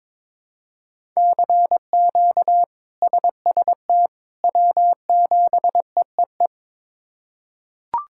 Morse code, heard as a keyed tone, about 22 wpm, 700 Hz. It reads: CQ SST W7EEE
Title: K1USN SST QSO Practice - 22wpm